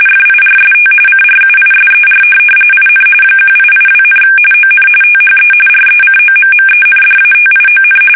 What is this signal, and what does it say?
RTTY, about 45.45 baud. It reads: FIRST CRUISER IN 1882, ON THE RECOMMENDATION
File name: fsk850 / usnavy3